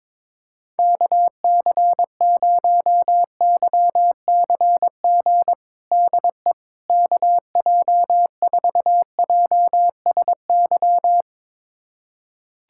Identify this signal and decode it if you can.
Morse code, heard as a keyed tone, about 22 wpm, 700 Hz. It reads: KC0YCG DE KJ4JSY
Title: USA Callsign DE Callsign - 6 Characters - 22wpm